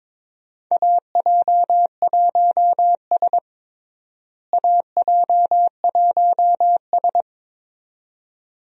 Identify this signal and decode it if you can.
Morse code, heard as a keyed tone, about 22 wpm, 700 Hz. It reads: AJ1S AJ1S
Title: USA Call Signs - 4 Characters - Repeated 2x - 22wpm